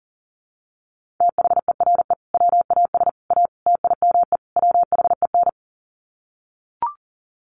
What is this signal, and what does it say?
Morse code, heard as a keyed tone, about 40 wpm, 700 Hz. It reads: THERE WAS A TIME WHEN
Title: Sets of 5 Words 40wpm